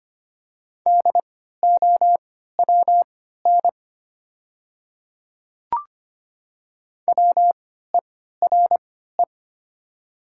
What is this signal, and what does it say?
Morse code, heard as a keyed tone, about 25 wpm, spaced as for 14 wpm, 700 Hz. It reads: DOWN WERE
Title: 3x Character Spacing - Top 100 Words - 25wpm